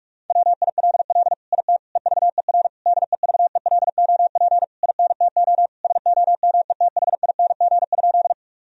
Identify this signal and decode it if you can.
Morse code, heard as a keyed tone, about 46 wpm, 700 Hz. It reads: WILL IT EVER DEVELOP INTO SOMETHING?
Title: Sentences from Top 1000 Words - Review Part 2 - 45wpm